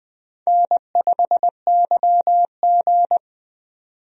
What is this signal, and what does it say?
Morse code, heard as a keyed tone, about 20 wpm, 700 Hz. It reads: N5YG
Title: USA Call Signs - 4 Characters - 20wpm